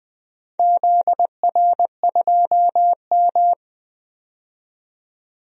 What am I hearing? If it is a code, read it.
Morse code, heard as a keyed tone, about 20 wpm, 700 Hz. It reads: ZR2M
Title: DXCC-100 Call Signs - 4 Characters - 20wpm